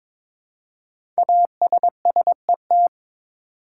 Morse code, heard as a keyed tone, about 22 wpm, 700 Hz. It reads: ASSET
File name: 5 Letter Words - First 3 Letters Given - 22wpm